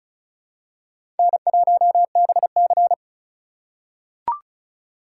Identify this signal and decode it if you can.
Morse code, heard as a keyed tone, about 36 wpm, 700 Hz. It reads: N1BC
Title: USA Call Signs - 4 Characters - 2x Character Spacing - 35wpm